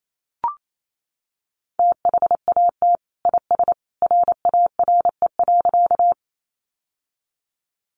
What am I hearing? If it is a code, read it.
Morse code, heard as a keyed tone, about 28 wpm, 700 Hz. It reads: THAT IS RARE.
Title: Sentences from Top 1100 Words 28wpm